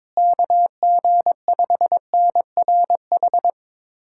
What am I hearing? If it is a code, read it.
Morse code, heard as a keyed tone, about 22 wpm, 700 Hz. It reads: KG5NRH
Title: USA Callsign DE Callsign - 6 Characters - 22wpm